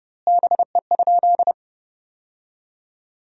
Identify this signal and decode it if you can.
Morse code, heard as a keyed tone, about 30 wpm, 700 Hz. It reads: BE?
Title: 3x Word-Spacing - Sentences from Top 100 Words - 30wpm